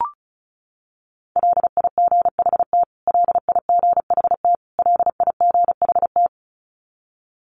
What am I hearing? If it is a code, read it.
Morse code, heard as a keyed tone, about 34 wpm, 700 Hz. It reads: LIGHT LIGHT LIGHT
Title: Top 500 Words Repeated 35wpm